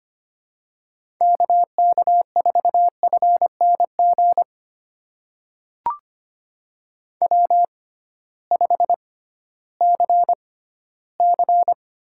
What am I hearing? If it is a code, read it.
Morse code, heard as a keyed tone, about 25 wpm, spaced as for 15 wpm, 700 Hz. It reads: KK4FNG W5CC
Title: USA Call Signs - 6x Character Spacing - 25wpm